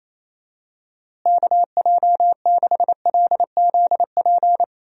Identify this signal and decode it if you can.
Morse code, heard as a keyed tone, about 28 wpm, 700 Hz. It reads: KJ6LZP
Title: USA Call Signs - 6 Characters - Repeated 2x - 28wpm